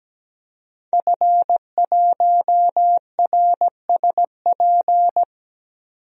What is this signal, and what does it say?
Morse code, heard as a keyed tone, about 17 wpm, 700 Hz. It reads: F1RSP
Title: DXCC-100 Call Signs - 5 Characters - 17wpm